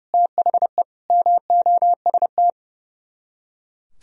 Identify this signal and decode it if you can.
Morse code, heard as a keyed tone, about 39 wpm, 700 Hz. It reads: THE MOST